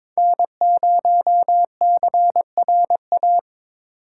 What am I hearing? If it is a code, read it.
Morse code, heard as a keyed tone, about 21 wpm, 700 Hz. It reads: N0CRA
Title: USA Call Signs - 5 Characters - 3x Character Spacing - 22wpm